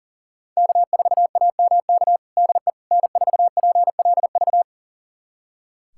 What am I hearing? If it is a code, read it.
Morse code, heard as a keyed tone, about 37 wpm, 700 Hz. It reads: K4AMK DE N4PLV
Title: USA Callsign DE Callsign - 5 Characters - 40wpm